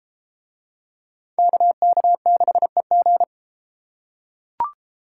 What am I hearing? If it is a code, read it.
Morse code, heard as a keyed tone, about 30 wpm, 700 Hz. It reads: KK6EG